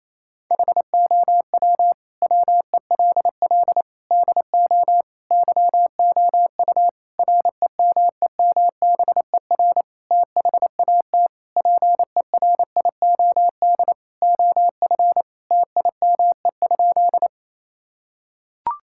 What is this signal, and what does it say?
Morse code, heard as a keyed tone, about 28 wpm, 700 Hz. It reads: HOW WELL DO YOU REMEMBER THAT PERIOD OF TIME?
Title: Sentences from Top 1000 Words - Review Part 5 - 28wpm